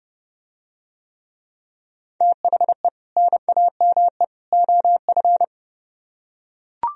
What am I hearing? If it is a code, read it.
Morse code, heard as a keyed tone, about 30 wpm, 700 Hz. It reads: THE NAME OF